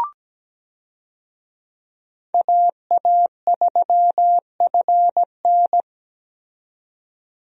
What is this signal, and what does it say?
Morse code, heard as a keyed tone, about 17 wpm, 700 Hz. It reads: AA3FN
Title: USA Call Signs - 5 Characters - 17wpm